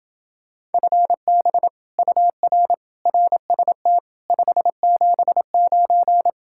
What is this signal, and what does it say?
Morse code, heard as a keyed tone, about 29 wpm, 700 Hz. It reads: FB UR RST 579